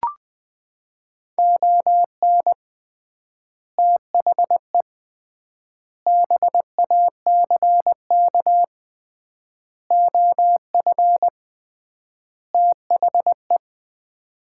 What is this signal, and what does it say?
Morse code, heard as a keyed tone, about 20 wpm, 700 Hz. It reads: ON THE BACK OF THE
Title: 3x Word-Spacing - Sets of 5 Words - 20wpm